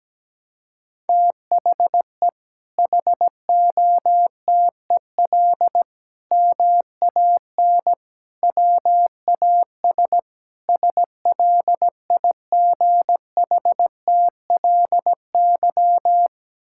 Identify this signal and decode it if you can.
Morse code, heard as a keyed tone, about 17 wpm, 700 Hz. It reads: THE HOTEL MAN WAS SLIGHTLY
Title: Sentences from Top 1600 Words 17wpm